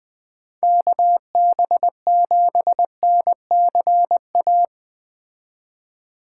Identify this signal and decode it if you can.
Morse code, heard as a keyed tone, about 20 wpm, 700 Hz. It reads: KB7NCA
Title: Speed Racing - USA Call Signs 17wpm